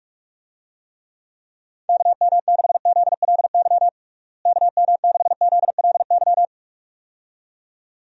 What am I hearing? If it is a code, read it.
Morse code, heard as a keyed tone, about 46 wpm, 700 Hz. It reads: KM6ZLY KM6ZLY
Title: USA Call Signs - 6 Characters - Repeated 2x - 45wpm